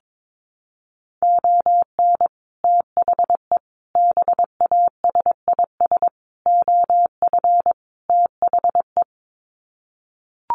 Morse code, heard as a keyed tone, about 22 wpm, 700 Hz. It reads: ON THE BASIS OF THE
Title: Sets of 5 Words 22wpm